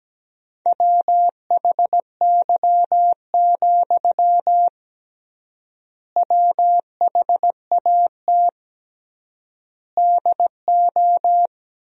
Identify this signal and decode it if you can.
Morse code, heard as a keyed tone, about 17 wpm, 700 Hz. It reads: WHY, WHAT DO
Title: 3x Word-Spacing - Sentences from Top 200 Words - 17wpm